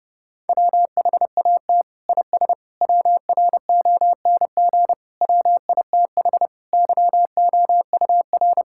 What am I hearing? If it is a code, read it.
Morse code, heard as a keyed tone, about 30 wpm, 700 Hz. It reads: WHAT IS WRONG WITH YOUR
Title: Sentences from Top 1000 Words - Review Part 5 - 30wpm